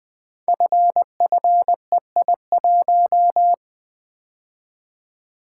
Morse code, heard as a keyed tone, about 20 wpm, 700 Hz. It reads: FFEI1
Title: Code Groups - 5 Letter:Numbers - 20wpm